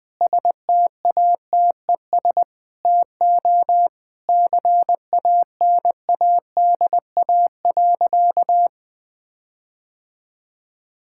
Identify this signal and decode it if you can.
Morse code, heard as a keyed tone, about 20 wpm, 700 Hz. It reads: STATES TO CANADA.
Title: Sentences with 4-Word Phrases - 20wpm